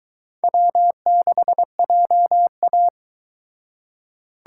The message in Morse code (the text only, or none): W6JA